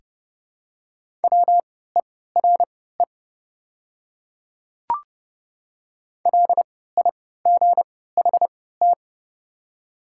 Morse code, heard as a keyed tone, about 30 wpm, spaced as for 17 wpm, 700 Hz. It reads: WERE LIGHT
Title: 3x Character Spacing - Top 200 Words - 30wpm